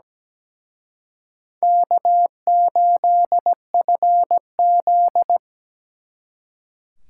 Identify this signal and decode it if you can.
Morse code, heard as a keyed tone, about 20 wpm, 700 Hz. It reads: K8FZ
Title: Speed Racing - USA 1x2 Call Signs 17wpm